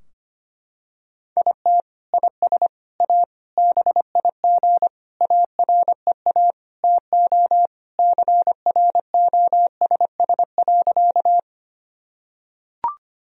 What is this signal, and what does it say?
Morse code, heard as a keyed tone, about 25 wpm, 700 Hz. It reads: IT IS A BIG AREA TO CROSS.
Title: Sentences from Top 1000 Words - Review Part 2 - 25wpm